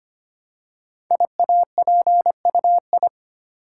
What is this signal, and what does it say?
Morse code, heard as a keyed tone, about 25 wpm, 700 Hz. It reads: IAPUI
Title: Code Groups - 5 Letters - 25wpm